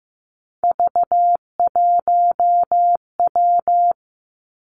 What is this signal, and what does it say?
Morse code, heard as a keyed tone, about 15 wpm, 700 Hz. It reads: V1W
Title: Three Challenging Letter-Number Combinations 15wpm